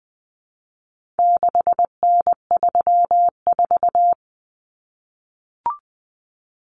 Morse code, heard as a keyed tone, about 20 wpm, 700 Hz. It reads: 6N34
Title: Common Cut Numbers - 4-Digit 20wpm